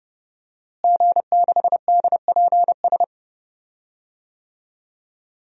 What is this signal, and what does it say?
Morse code, heard as a keyed tone, about 31 wpm, 700 Hz. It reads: G6DPS
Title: DXCC-100 Call Signs - 5 Characters - Repeated 2x - 30wpm